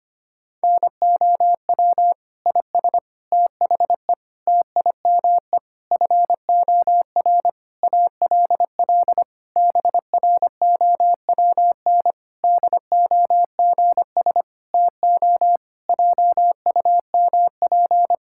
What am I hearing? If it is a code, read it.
Morse code, heard as a keyed tone, about 25 wpm, 700 Hz. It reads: NOW IS THE TIME FOR ALL BROWN DOGS TO JUMP
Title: Pangram Challenge v2 - 25wpm